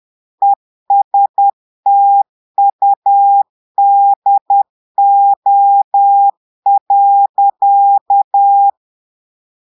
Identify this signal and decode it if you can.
Morse code, heard as a keyed tone, about 10 wpm, 800 Hz. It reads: ESTUDO.
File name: Portugais - texte en clair 1 - vitesse - 10 mots minute